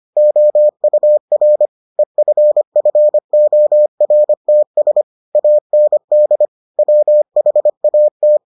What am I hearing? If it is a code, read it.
Morse code, heard as a keyed tone, about 25 wpm, 600 Hz. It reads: OUR EFFORTS AND WHAT